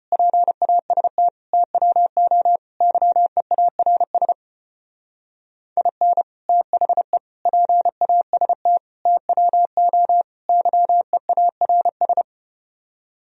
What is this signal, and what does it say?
Morse code, heard as a keyed tone, about 33 wpm, 700 Hz. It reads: PAST TWO YEARS IN THE PAST TWO YEARS